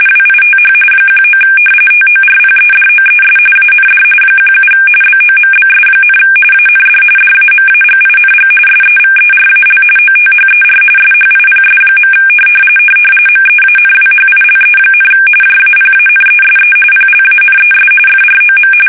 RTTY, about 45.45 baud. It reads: WITH 19 TORPEDOES AND 17 BOMBS, AND DAMAGING SEVERAL OTHER SHIPS. MEANWHILE, JAPANESE PLANES FROM
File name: fsk850 / battleshipnj